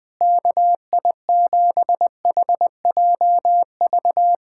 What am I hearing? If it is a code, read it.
Morse code, heard as a keyed tone, about 20 wpm, 700 Hz. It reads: KI7HJV
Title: USA Callsign DE Callsign - 6 Characters - 20wpm